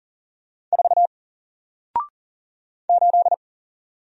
Morse code, heard as a keyed tone, about 40 wpm, 700 Hz. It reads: 4 8
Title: Single Letter-Number Alt 40wpm